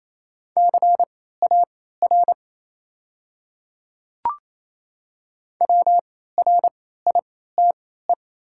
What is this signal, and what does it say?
Morse code, heard as a keyed tone, about 27 wpm, spaced as for 16 wpm, 700 Hz. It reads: CAR WRITE
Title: 3x Character Spacing - Top 500 Words - 28wpm